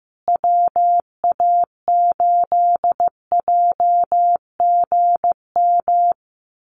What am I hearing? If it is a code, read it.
Morse code, heard as a keyed tone, about 15 wpm, 700 Hz. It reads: WA8JGM
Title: USA Call Signs Repeated 2x 15wpm